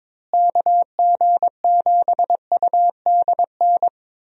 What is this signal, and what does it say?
Morse code, heard as a keyed tone, about 22 wpm, 700 Hz. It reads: KG7UDN
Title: USA Call Signs - 6 Characters - Repeated 3x - 22wpm